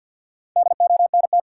Morse code, heard as a keyed tone, about 69 wpm, spaced as for 59 wpm, 700 Hz. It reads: DONT
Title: Speed-Racing Top 180 QSO Elements 50wpm